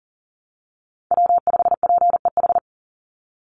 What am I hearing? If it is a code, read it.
Morse code, heard as a keyed tone, about 39 wpm, 700 Hz. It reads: W5PEH
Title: Call Signs Repeated 3x - 40wpm